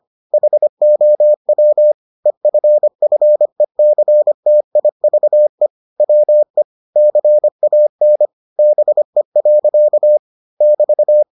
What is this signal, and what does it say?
Morse code, heard as a keyed tone, about 25 wpm, 600 Hz. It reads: HOW EFFECTIVE WE CAN BE. <BT>